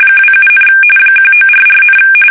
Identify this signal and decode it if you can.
RTTY, about 45.45 baud. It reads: 27, 1944,
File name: fsk850 / battleshipnj